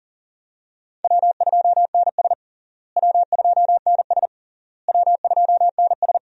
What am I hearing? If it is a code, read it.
Morse code, heard as a keyed tone, about 40 wpm, 700 Hz. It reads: W2NS W2NS W2NS
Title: USA Call Signs - 4 Characters - Repeated 3x - 40wpm